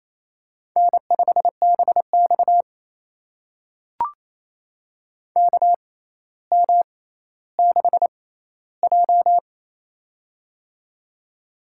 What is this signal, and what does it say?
Morse code, heard as a keyed tone, about 28 wpm, spaced as for 14 wpm, 700 Hz. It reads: N5BX KM6J
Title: USA Call Signs - 4 Characters - 6x Character Spacing - 28wpm